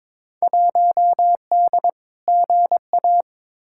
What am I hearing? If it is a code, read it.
Morse code, heard as a keyed tone, about 21 wpm, 700 Hz. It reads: 1D GA
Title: Field Day Exchanges - 22wpm